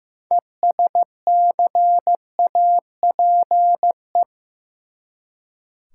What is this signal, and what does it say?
Morse code, heard as a keyed tone, about 15 wpm, 700 Hz. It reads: ESCAPE
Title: Six Letter Words - 15wpm